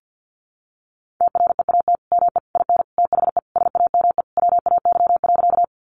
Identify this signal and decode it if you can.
Morse code, heard as a keyed tone, about 50 wpm, 700 Hz. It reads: TREAT ME IN THE SAME WAY.
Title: Sentences from Top 1300 Words 50wpm